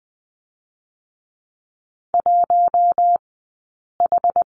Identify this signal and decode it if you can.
Morse code, heard as a keyed tone, about 20 wpm, 700 Hz. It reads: 1 H
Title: 066 Lesson 12d - Letter d - Two Characters 20wpm